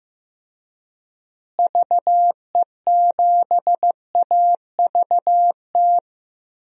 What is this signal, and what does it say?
Morse code, heard as a keyed tone, about 15 wpm, 700 Hz. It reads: VE7AVT
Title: DXCC-100 Call Signs - 6 Characters - Repeated 3x - 15wpm